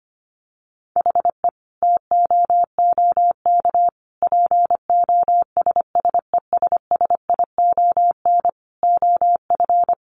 Morse code, heard as a keyed tone, about 25 wpm, 700 Hz. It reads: HE TOOK POSSESSION OF
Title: Sentences from Top 1500 Words 25wpm